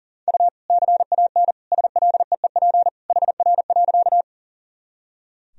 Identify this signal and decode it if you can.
Morse code, heard as a keyed tone, about 40 wpm, 700 Hz. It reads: U CAN SLEEP HR.
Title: Sentences from Top 500 Words and Abbreviations 40wpm